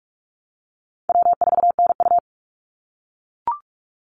Morse code, heard as a keyed tone, about 44 wpm, 700 Hz. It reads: W4NU
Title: Easy Call Signs - 45wpm